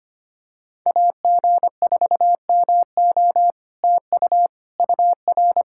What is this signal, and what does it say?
Morse code, heard as a keyed tone, about 25 wpm, 700 Hz. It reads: AG4MO TU UR
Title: SOTA QSO Practice 25wpm